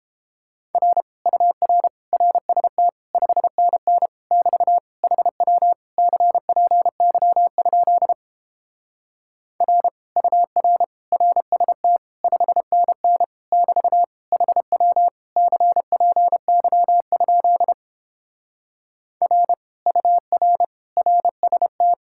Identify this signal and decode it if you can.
Morse code, heard as a keyed tone, about 31 wpm, 700 Hz. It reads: R UR RST 5NN <BT> HW CPY? R UR RST 5NN <BT> HW CPY? R UR RST